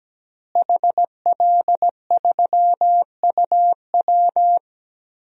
Morse code, heard as a keyed tone, about 17 wpm, 700 Hz. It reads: HL3UW
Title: DXCC-100 Call Signs - 5 Characters - Repeated 2x - 17wpm